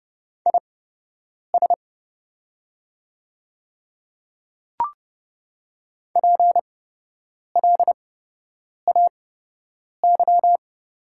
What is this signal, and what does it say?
Morse code, heard as a keyed tone, about 30 wpm, spaced as for 8 wpm, 690 Hz. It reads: IS PLAY
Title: 8x Character Spacing - Top 300 Words - 30wpm